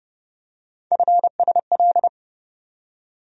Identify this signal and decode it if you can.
Morse code, heard as a keyed tone, about 30 wpm, 700 Hz. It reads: FSL
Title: USA Call Sign Postfixes - 3 Letters 30wpm